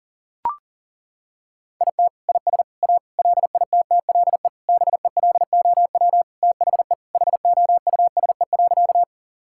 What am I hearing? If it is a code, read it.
Morse code, heard as a keyed tone, about 40 wpm, 700 Hz. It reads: IT IS A LITTLE BELOW THE HOUSE.
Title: Sentences from Top 1000 Words - Review Part 5 - 40wpm